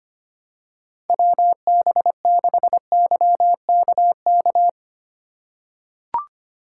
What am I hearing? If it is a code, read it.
Morse code, heard as a keyed tone, about 25 wpm, 700 Hz. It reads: WB6YKK
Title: USA Call Signs - 4x Character Spacing - 25wpm